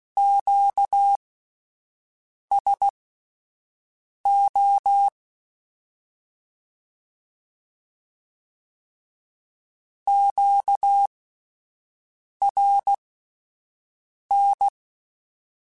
Morse code, heard as a keyed tone, about 16 wpm, spaced as for 5 wpm, 800 Hz. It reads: QSO QRN
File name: (spaced) / Q-Calls Spaced-mp3s / Q Calls Twice sp @10